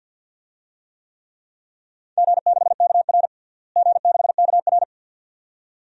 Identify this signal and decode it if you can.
Morse code, heard as a keyed tone, about 52 wpm, 700 Hz. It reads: G6KR G6KR
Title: DXCC-100 Call Signs - 4 Characters - Repeated 2x - 50wpm